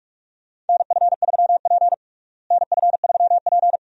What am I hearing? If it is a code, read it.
Morse code, heard as a keyed tone, about 42 wpm, 700 Hz. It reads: NF3P NF3P
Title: USA Call Signs - 4 Characters - Repeated 2x - 45wpm